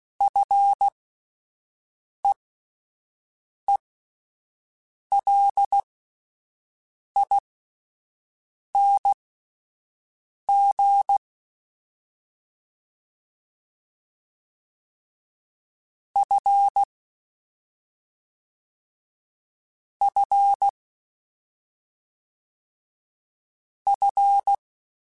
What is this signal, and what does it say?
Morse code, heard as a keyed tone, about 16 wpm, spaced as for 5 wpm, 800 Hz. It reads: FEELING F F F